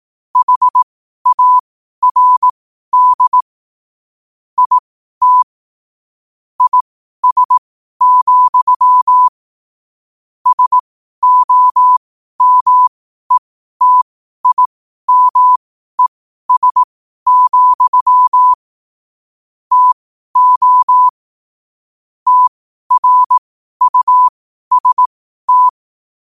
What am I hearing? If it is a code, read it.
Morse code, heard as a keyed tone, about 18 wpm, spaced as for 12 wpm, 1000 Hz. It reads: HARD IT IS, SOMETIMES, TO TRUST
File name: qotdfiles / 12wpm